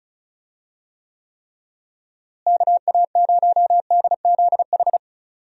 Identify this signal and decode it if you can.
Morse code, heard as a keyed tone, about 36 wpm, 700 Hz. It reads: KA0DZH
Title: USA Call Signs - 6 Characters - Repeated 3x - 35wpm